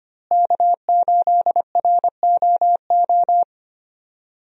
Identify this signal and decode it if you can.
Morse code, heard as a keyed tone, about 25 wpm, 700 Hz. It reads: K8ROO
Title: USA Call Signs - 5 Characters - Repeated 3x - 25wpm